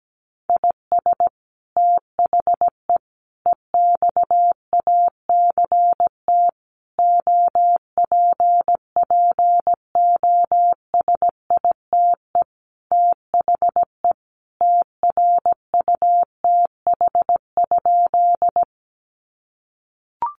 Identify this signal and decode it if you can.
Morse code, heard as a keyed tone, about 17 wpm, 700 Hz. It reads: IS THE EXACT OPPOSITE THE TRUTH?
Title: Sentences from Top 1300 Words 17wpm